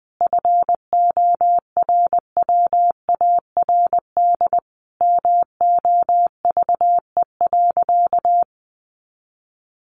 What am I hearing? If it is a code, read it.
Morse code, heard as a keyed tone, about 20 wpm, 700 Hz. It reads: FORWARD MOVE.
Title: Sentences from Top 1300 Words 20wpm